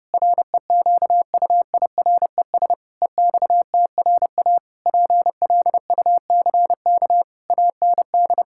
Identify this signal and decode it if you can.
Morse code, heard as a keyed tone, about 32 wpm, 700 Hz. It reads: REQUIRES EXTRA PLUCK AND
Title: Pangram Challenge v2 - 30wpm